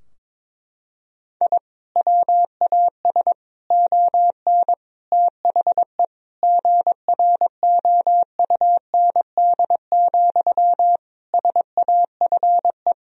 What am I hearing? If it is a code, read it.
Morse code, heard as a keyed tone, about 22 wpm, 700 Hz. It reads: I WAS ON THE GROUND, SAFE
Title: Sentences from Top 1000 Words - Review Part 2 - 22wpm